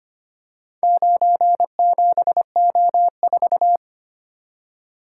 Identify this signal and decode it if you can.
Morse code, heard as a keyed tone, about 25 wpm, 700 Hz. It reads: 97O4
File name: practice / 25wpm / Four Letter-Number Combinations - 25wpm